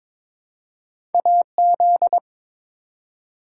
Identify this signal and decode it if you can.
Morse code, heard as a keyed tone, about 22 wpm, 700 Hz. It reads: AZ
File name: ARRL Sections v2 - 22wpm